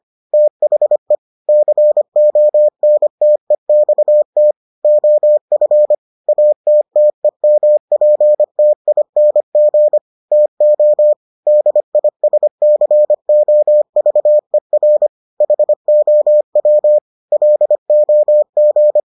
Morse code, heard as a keyed tone, about 25 wpm, 600 Hz. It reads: THE CONTEXT OF ATTEMPTING TO DISCOVER HOW LOG